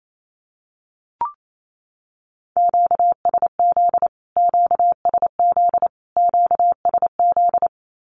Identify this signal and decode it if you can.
Morse code, heard as a keyed tone, about 28 wpm, 700 Hz. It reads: QSZ QSZ QSZ
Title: Q-Codes Repeated 3x 28wpm